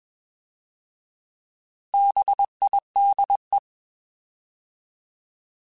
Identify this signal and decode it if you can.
Morse code, heard as a keyed tone, about 21 wpm, 800 Hz. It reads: BIDE
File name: Bs @20